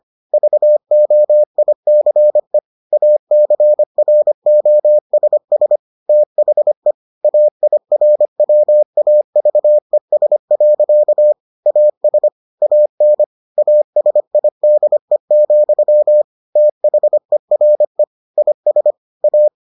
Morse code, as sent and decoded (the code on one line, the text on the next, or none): ...- --- .. -.-. / .- -.-. .-. --- ... ... / - .... / .- .. .-. .-- .- ...- . ... .-.-.- / .- ... / .- -. / .- ... .. -.. --..-- / - .... .-. / .. ... / .-
VOICE ACROSS THE AIRWAVES. AS AN ASIDE, THERE IS A